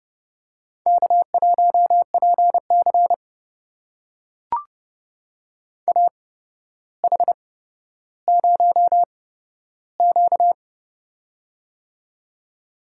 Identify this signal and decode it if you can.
Morse code, heard as a keyed tone, about 30 wpm, spaced as for 13 wpm, 700 Hz. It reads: K1PC AH0Q
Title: USA Call Signs - 8x Character Spacing - 30wpm